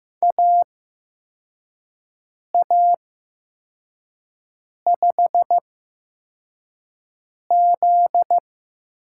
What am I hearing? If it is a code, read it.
Morse code, heard as a keyed tone, about 15 wpm, spaced as for 4 wpm, 700 Hz. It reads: AA5Z
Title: USA Call Signs - 4 Characters - 8x Character Spacing - 15wpm